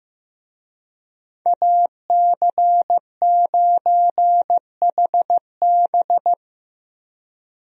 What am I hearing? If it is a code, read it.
Morse code, heard as a keyed tone, about 15 wpm, 700 Hz. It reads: AC9HB
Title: USA Call Signs - 5 Characters - Repeated 3x - 15wpm